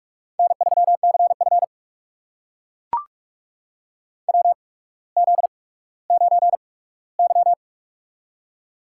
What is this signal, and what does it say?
Morse code, heard as a keyed tone, about 44 wpm, spaced as for 19 wpm, 690 Hz. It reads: N3CF WZ9Y
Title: USA Call Signs - 4 Characters - 8x Character Spacing - 45wpm